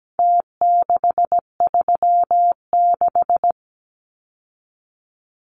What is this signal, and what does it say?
Morse code, heard as a keyed tone, about 17 wpm, 700 Hz. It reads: T636
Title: Numbers - Contest Style - 4-Digit 17wpm